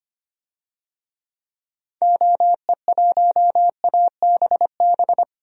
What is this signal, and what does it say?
Morse code, heard as a keyed tone, about 25 wpm, 700 Hz. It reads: OE1ABB
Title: DXCC-100 Call Signs - 6 Characters - 25wpm